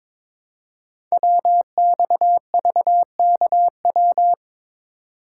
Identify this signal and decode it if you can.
Morse code, heard as a keyed tone, about 22 wpm, 700 Hz. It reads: WXVKW
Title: Code Groups - 5 Letters - 22wpm